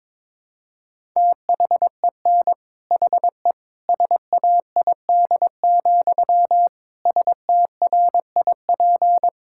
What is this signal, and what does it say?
Morse code, heard as a keyed tone, about 22 wpm, 700 Hz. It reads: THEN HE SAID, STRIP
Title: Sentences from Top 1600 Words 22wpm